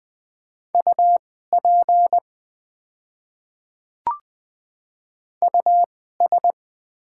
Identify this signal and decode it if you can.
Morse code, heard as a keyed tone, about 20 wpm, spaced as for 14 wpm, 700 Hz. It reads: UP US
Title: 2x Character Spacing - Top 200 Words - 20wpm